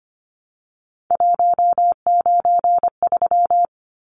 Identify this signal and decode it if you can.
Morse code, heard as a keyed tone, about 25 wpm, 700 Hz. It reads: 193
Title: Numbers - 3-Digit 25wpm